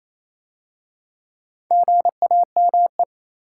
Morse code, heard as a keyed tone, about 27 wpm, 700 Hz. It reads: GAME
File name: Easy Words - Speed-Racing with Hints - 22wpm